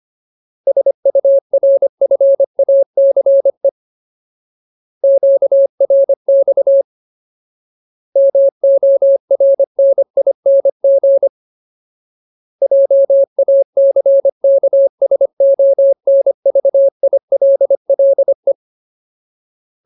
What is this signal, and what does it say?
Morse code, heard as a keyed tone, about 25 wpm, 550 Hz. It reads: SURFACE QRX MORNING JACKSONVILLE
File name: Ultimate Word List / 25WPM Ultimate